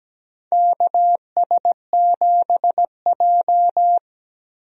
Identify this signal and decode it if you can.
Morse code, heard as a keyed tone, about 17 wpm, 700 Hz. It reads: KS7J
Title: USA Call Signs - 4 Characters - Repeated 2x - 17wpm